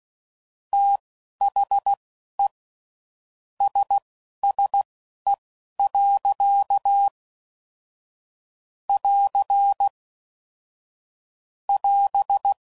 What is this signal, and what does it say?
Morse code, heard as a keyed tone, about 16 wpm, spaced as for 11 wpm, 800 Hz. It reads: THE SSE. <AR> <AS>
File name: CW Code Groups @10WPM / Weather Sets / Weather 4 14